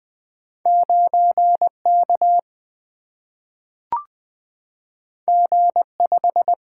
Morse code, heard as a keyed tone, about 20 wpm, 700 Hz. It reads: 9 K G 5
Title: Single-Letter Single-Number 20wpm